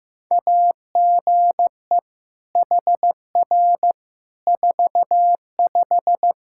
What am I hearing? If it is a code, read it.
Morse code, heard as a keyed tone, about 15 wpm, 700 Hz. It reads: AGE HR 45
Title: Practice QSO - Exchange 3 - More Info - 15wpm